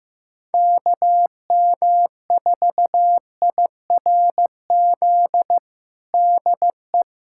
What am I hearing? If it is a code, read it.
Morse code, heard as a keyed tone, about 15 wpm, 700 Hz. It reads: KM4IRZ DE
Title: USA Callsign DE Callsign - 6 Characters - 15wpm